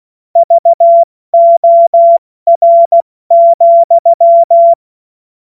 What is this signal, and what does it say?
Morse code, heard as a keyed tone, about 16 wpm, 680 Hz. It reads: VOR,
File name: Morsekurs / Witze / Witz 19. Oktober 2021, 70 F80